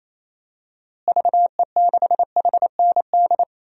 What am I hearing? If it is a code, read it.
Morse code, heard as a keyed tone, about 28 wpm, 700 Hz. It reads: VE6HND
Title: DXCC-100 Call Signs - 6 Characters - Repeated 2x - 28wpm